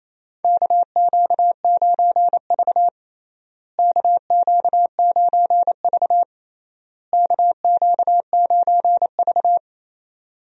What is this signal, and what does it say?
Morse code, heard as a keyed tone, about 28 wpm, 700 Hz. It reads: KQ9V KQ9V KQ9V
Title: USA Call Signs - 4 Characters - Repeated 3x - 28wpm